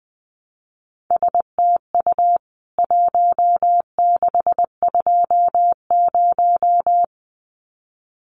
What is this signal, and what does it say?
Morse code, heard as a keyed tone, about 20 wpm, 700 Hz. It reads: STU 1620
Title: CWT Exchange 20wpm